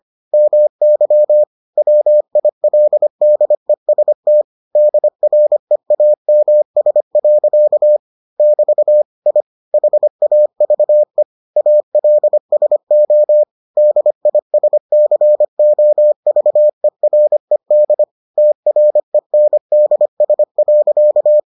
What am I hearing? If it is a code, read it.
Morse code, heard as a keyed tone, about 25 wpm, 600 Hz. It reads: MY WILDEST DREAMS. <BT> I HAVE ALSO DISCOVERED TRENDS.